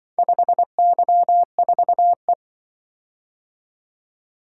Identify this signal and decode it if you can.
Morse code, heard as a keyed tone, about 25 wpm, 700 Hz. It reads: 5Y4E
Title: Speed-Racing - Four Letter-Number Combinations - 20wpm